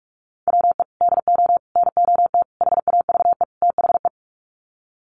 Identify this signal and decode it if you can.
Morse code, heard as a keyed tone, about 44 wpm, 700 Hz. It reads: WE DO NOT HAVE THE
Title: Sets of 5 Words 45wpm